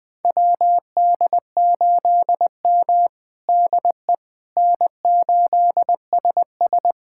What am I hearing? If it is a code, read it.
Morse code, heard as a keyed tone, about 20 wpm, 700 Hz. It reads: WD8M DE N8SS
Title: USA Callsign DE Callsign - 4 Characters - 20wpm